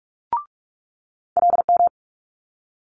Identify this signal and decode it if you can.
Morse code, heard as a keyed tone, about 46 wpm, 700 Hz. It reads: LM
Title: Multi-letter Phonemes Quick Fire 45wpm